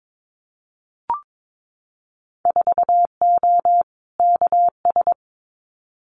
Morse code, heard as a keyed tone, about 22 wpm, 700 Hz. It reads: 4O KS
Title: Winter Field Day Exchanges Encore 22wpm